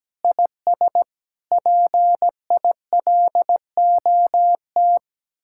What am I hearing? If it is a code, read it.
Morse code, heard as a keyed tone, about 17 wpm, 700 Hz. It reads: IS PILOT
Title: Speed-Racing Common QSO Phrases Encore - 17wpm